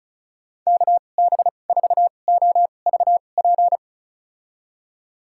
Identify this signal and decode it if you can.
Morse code, heard as a keyed tone, about 36 wpm, spaced as for 25 wpm, 700 Hz. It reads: KB4OVP
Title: USA Call Signs - 6 Characters - 2x Character Spacing - 35wpm